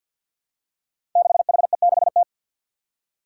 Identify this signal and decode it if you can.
Morse code, heard as a keyed tone, about 53 wpm, 700 Hz. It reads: 6HE6T
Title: Five Challenging Letter-Number Combinations 50wpm